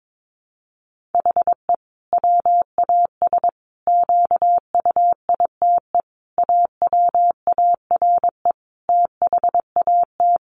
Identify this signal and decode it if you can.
Morse code, heard as a keyed tone, about 22 wpm, 700 Hz. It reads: HE WAS QUITE AWARE THAT
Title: Sentences from Top 1400 Words 22wpm